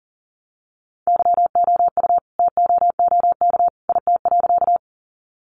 Morse code, heard as a keyed tone, about 40 wpm, 700 Hz. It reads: YOU TOOK IT.
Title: Sentences from Top 300 Words 40wpm